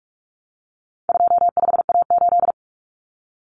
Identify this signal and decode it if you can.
Morse code, heard as a keyed tone, about 44 wpm, 700 Hz. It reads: 25A8
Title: Common Cut Numbers - 4-Digit 45wpm